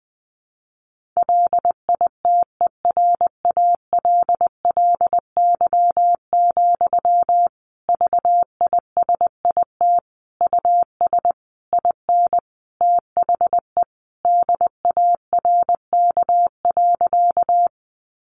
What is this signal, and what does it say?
Morse code, heard as a keyed tone, about 20 wpm, 700 Hz. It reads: LITERALLY, VISIT US IN THE DARK.
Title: Sentences from Top 1400 Words 20wpm